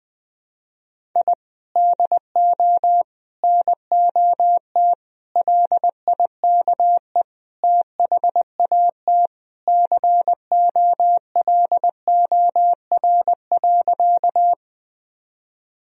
Morse code, heard as a keyed tone, about 21 wpm, 700 Hz. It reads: I DO NOT LIKE THAT COLOR.
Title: Sentences from Top 1000 Words - Review Part 2 - 20wpm